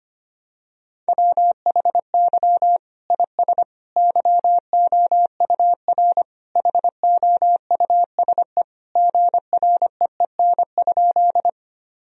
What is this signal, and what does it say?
Morse code, heard as a keyed tone, about 25 wpm, 700 Hz. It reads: WHY IS YOUR HOUSE GREEN?
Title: Sentences from Top 1000 Words - Review Part 2 - 25wpm